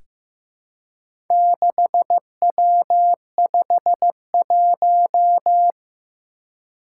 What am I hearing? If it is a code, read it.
Morse code, heard as a keyed tone, about 15 wpm, 700 Hz. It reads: 6W51
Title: Four Challenging Letter-Number Combinations 15wpm